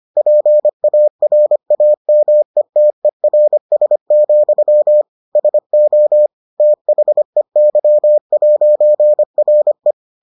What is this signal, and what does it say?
Morse code, heard as a keyed tone, about 25 wpm, 600 Hz. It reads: PARAMETERS, SO THEY'RE